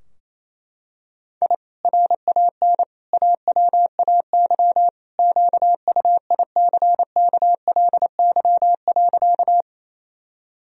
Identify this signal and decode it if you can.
Morse code, heard as a keyed tone, about 28 wpm, 700 Hz. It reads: I RAN AWAY QUICKLY.
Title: Sentences from Top 1000 Words - Review Part 3 - 28wpm